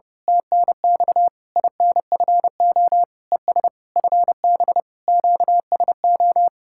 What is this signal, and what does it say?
Morse code, heard as a keyed tone, about 28 wpm, 700 Hz. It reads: TNX INFO ES FB QSO
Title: Speed-Racing Common QSO Phrases - 30wpm